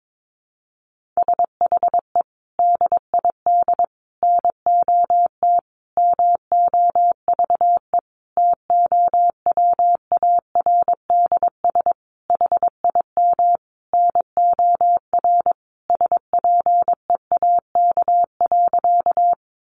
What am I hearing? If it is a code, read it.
Morse code, heard as a keyed tone, about 22 wpm, 700 Hz. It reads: SHE DID NOT MOVE TOWARDS HIM NOR SPEAK.
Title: Sentences from Top 1500 Words 22wpm